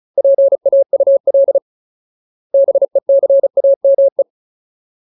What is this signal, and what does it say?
Morse code, heard as a keyed tone, about 34 wpm, 550 Hz. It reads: PAUL BECAME
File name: Ultimate Word List / 35WPM Ultimate